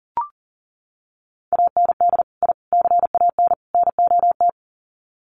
Morse code, heard as a keyed tone, about 40 wpm, 700 Hz. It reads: AND I CAN NOT
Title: Sets of 4 Words 40wpm